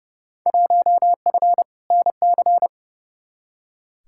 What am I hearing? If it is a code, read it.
Morse code, heard as a keyed tone, about 30 wpm, 700 Hz. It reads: 1F NC
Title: Field Day Exchanges - 30wpm